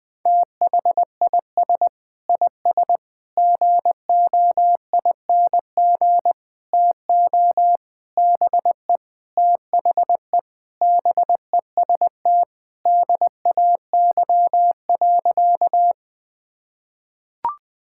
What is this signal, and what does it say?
Morse code, heard as a keyed tone, about 20 wpm, 700 Hz. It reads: THIS IS GOING TO BE THE BEST DAY.
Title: Sentences with 4-Word Phrases - 20wpm